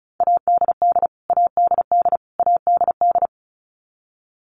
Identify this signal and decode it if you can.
Morse code, heard as a keyed tone, about 34 wpm, 700 Hz. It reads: ADD ADD ADD
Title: Top 500 Words Repeated 35wpm